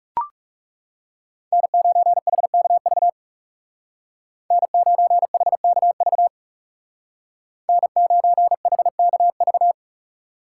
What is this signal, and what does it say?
Morse code, heard as a keyed tone, about 36 wpm, 700 Hz. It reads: N9HKV N9HKV N9HKV
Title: Speed Racing - USA 1x3 Call Signs 30wpm